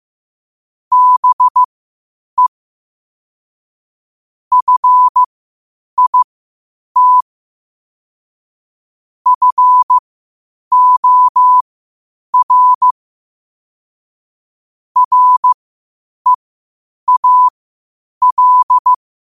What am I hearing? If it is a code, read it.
Morse code, heard as a keyed tone, about 15 wpm, spaced as for 8 wpm, 1000 Hz. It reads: BE FIT FOR REAL
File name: qotdfiles / 8wpm